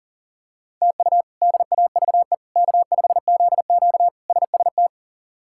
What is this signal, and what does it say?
Morse code, heard as a keyed tone, about 40 wpm, 700 Hz. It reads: TU DAVE K5ZQ SST
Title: K1USN SST QSO Practice - 40wpm